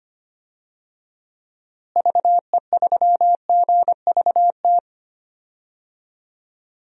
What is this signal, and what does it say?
Morse code, heard as a keyed tone, about 25 wpm, 700 Hz. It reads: VE3GVT
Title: DXCC-100 Call Signs - 6 Characters - Repeated 3x - 25wpm